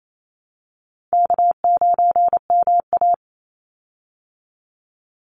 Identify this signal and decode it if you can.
Morse code, heard as a keyed tone, about 28 wpm, 700 Hz. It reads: K9MA
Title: Call Signs Repeated 3x - 28wpm